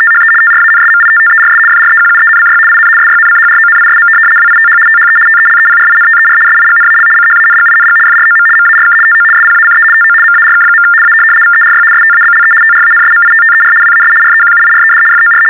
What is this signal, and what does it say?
RTTY, about 75 baud. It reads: THE TRANSMITTERS FREQUENCY IN SYNCHRONISM WITH THE TELEPRINTER SIGNAL CHANGING FROM MARK TO SPACE TO MARK. A VERY STABLE TRANSMITTER WAS REQUIRED FOR
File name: teletype-RTTY-75